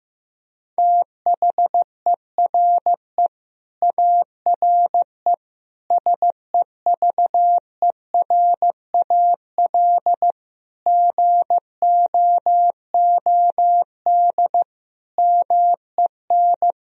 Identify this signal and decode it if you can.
Morse code, heard as a keyed tone, about 15 wpm, 700 Hz. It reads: THERE ARE SEVERAL GOOD MEN
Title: Sentences from Top 1000 Words - Review Part 2 - 15wpm